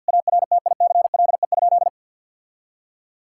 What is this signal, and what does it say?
Morse code, heard as a keyed tone, about 52 wpm, 700 Hz. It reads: ARTICLE?
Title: Sentences from Top 1000 Words - Review Part 5 - 50wpm